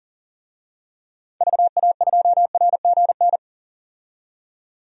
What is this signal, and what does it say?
Morse code, heard as a keyed tone, about 40 wpm, 700 Hz. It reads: VA2RGN
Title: DXCC-100 Call Signs - 6 Characters - 40wpm